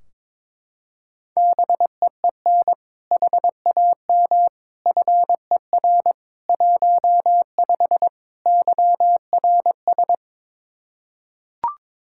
Morse code, heard as a keyed tone, about 22 wpm, 700 Hz. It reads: BEEN HAM FER 15 YRS
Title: K1USN SST QSO Practice 22wpm